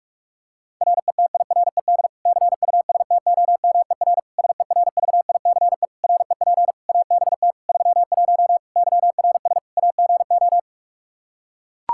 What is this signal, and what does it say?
Morse code, heard as a keyed tone, about 41 wpm, 700 Hz. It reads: RETIRED CUSTOMER SERVICE REP ABT 31 YRS AGO
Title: Practice QSO - Exchange 3 - More Info - 45wpm